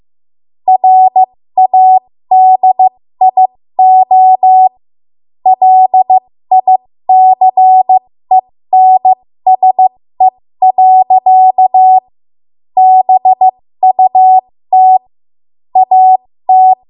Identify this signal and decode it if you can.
Morse code, heard as a keyed tone, about 15 wpm, spaced as for 13 wpm, 750 Hz. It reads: RADIO LICENSE. BUT AT
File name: Morse / 13WPM